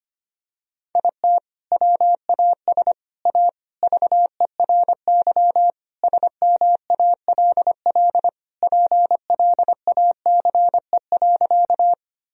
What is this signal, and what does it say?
Morse code, heard as a keyed tone, about 25 wpm, 700 Hz. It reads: IT WAS A VERY SMALL PLACE.
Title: Sentences from Top 1000 Words - Review Part 1 - 25wpm